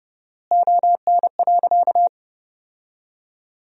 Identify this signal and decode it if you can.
Morse code, heard as a keyed tone, about 30 wpm, 700 Hz. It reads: ON.
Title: 4x Word-Spacing - Sentences from Top 100 Words - 30wpm